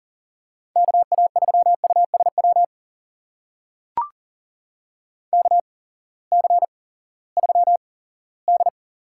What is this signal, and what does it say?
Morse code, heard as a keyed tone, about 40 wpm, spaced as for 20 wpm, 700 Hz. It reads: KA3USW KC3D
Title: USA Call Signs - 6 Characters - 8x Character Spacing - 40wpm